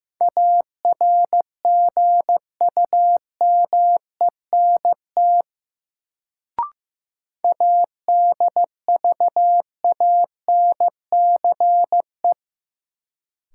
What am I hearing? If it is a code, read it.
Morse code, heard as a keyed tone, about 15 wpm, 700 Hz. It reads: ARGUMENT ADVANCE
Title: Sentences from Top 1400 Words - New Word Drill - 15wpm